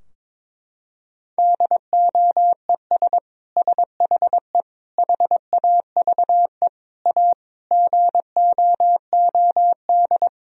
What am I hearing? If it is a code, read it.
Morse code, heard as a keyed tone, about 22 wpm, 700 Hz. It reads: DOES SHE HAVE A GOOD
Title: Sentences from Top 1000 Words - Review Part 2 - 22wpm